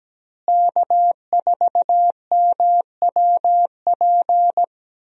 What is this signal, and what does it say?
Morse code, heard as a keyed tone, about 17 wpm, 700 Hz. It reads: K4MWP
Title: USA Call Signs - 5 Characters - Repeated 3x - 17wpm